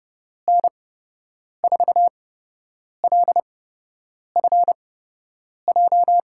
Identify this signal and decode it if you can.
Morse code, heard as a keyed tone, about 30 wpm, spaced as for 8 wpm, 700 Hz. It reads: N4LFJ
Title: USA Call Signs - 6 Characters - 8x Character Spacing - 30wpm